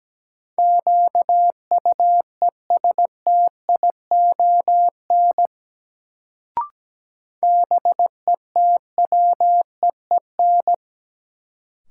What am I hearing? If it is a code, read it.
Morse code, heard as a keyed tone, about 17 wpm, 700 Hz. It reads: QUESTION BETWEEN
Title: Sentences from Top 400 Words - New Word Drill - 17wpm